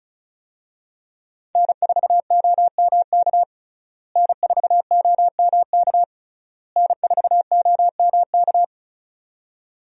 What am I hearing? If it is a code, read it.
Morse code, heard as a keyed tone, about 34 wpm, 700 Hz. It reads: N4OMK N4OMK N4OMK
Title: USA Call Signs - 5 Characters - Repeated 3x - 35wpm